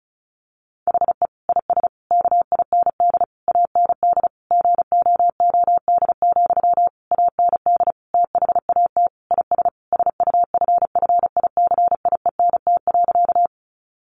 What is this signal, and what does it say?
Morse code, heard as a keyed tone, about 34 wpm, 700 Hz. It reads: HE IS KIND AND GOOD, AND THAT IS SUFFICIENT.
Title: Sentences from Top 1400 Words 35wpm